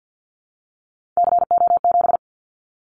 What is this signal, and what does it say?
Morse code, heard as a keyed tone, about 50 wpm, 700 Hz. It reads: CO7
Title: Callsign Prefixes for Top 100 DXCC - 50wpm